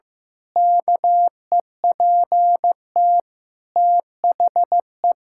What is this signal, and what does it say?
Morse code, heard as a keyed tone, about 15 wpm, 700 Hz. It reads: KEPT THE
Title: Pangram Challenge v2 - 15wpm